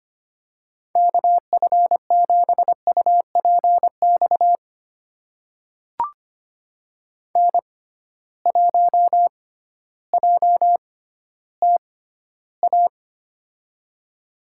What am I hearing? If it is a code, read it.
Morse code, heard as a keyed tone, about 25 wpm, spaced as for 13 wpm, 700 Hz. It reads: KF7UPX N1JTA
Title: USA Call Signs - 6x Character Spacing - 25wpm